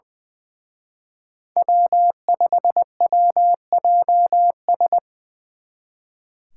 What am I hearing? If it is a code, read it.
Morse code, heard as a keyed tone, about 25 wpm, 700 Hz. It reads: W5WJS
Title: Speed Racing - USA 1x3 Call Signs 20wpm